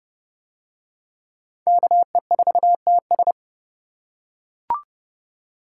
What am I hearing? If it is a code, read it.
Morse code, heard as a keyed tone, about 28 wpm, 700 Hz. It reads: KE4TS